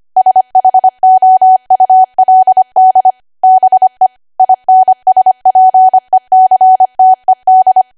Morse code, heard as a keyed tone, about 25 wpm, 750 Hz. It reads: SHOULD BE INSPECTED
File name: Archive / 25 WPM / 25WPM